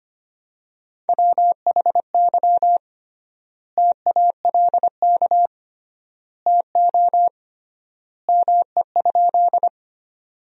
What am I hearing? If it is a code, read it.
Morse code, heard as a keyed tone, about 25 wpm, 700 Hz. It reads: WHY TALK TO ME?
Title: 3x Word-Spacing - Sentences from Top 400 Words - 25wpm